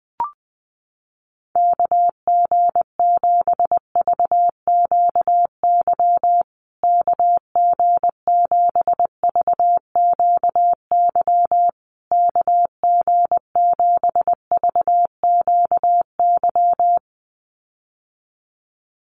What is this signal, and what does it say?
Morse code, heard as a keyed tone, about 20 wpm, 700 Hz. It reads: KG7VQY KG7VQY KG7VQY
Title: USA Call Signs Repeated 3x 20wpm